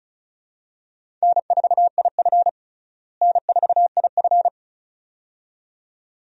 Morse code, heard as a keyed tone, about 34 wpm, 700 Hz. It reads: N4IF N4IF
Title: USA Call Signs - 4 Characters - Repeated 2x - 35wpm